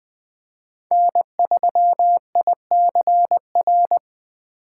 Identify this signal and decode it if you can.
Morse code, heard as a keyed tone, about 20 wpm, 700 Hz. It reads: N3ICR